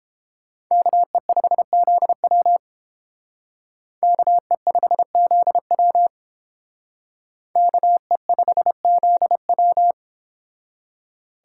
Morse code, heard as a keyed tone, about 31 wpm, 700 Hz. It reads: KE5ZW KE5ZW KE5ZW